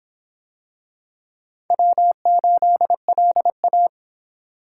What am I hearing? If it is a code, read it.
Morse code, heard as a keyed tone, about 25 wpm, 700 Hz. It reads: W8LA